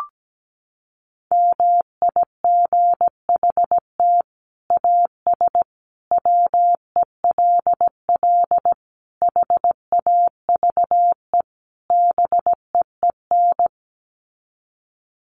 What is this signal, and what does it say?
Morse code, heard as a keyed tone, about 17 wpm, 700 Hz. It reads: MIGHT AS WELL HAVE BEEN
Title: Sets of 5 Words Encore2 17wpm